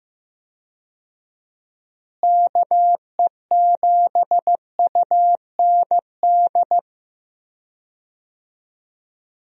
Morse code, heard as a keyed tone, about 15 wpm, 700 Hz. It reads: KE7UND
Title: USA Call Signs - 6 Characters - Repeated 3x - 15wpm